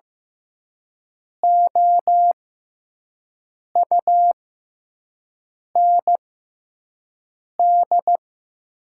Morse code, heard as a keyed tone, about 15 wpm, spaced as for 5 wpm, 700 Hz. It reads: OUND
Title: 6x Character Spacing - Top 300 Words - 15wpm